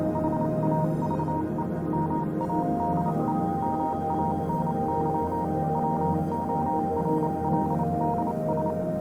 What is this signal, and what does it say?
Morse code, heard as a keyed tone, about 28 wpm, spaced as for 25 wpm, 940 Hz, with noise. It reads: HAVE MADE MANY WORLDS